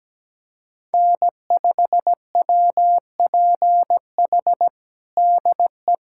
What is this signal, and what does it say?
Morse code, heard as a keyed tone, about 17 wpm, 700 Hz. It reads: N5WPH DE
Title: USA Callsign DE Callsign 17wpm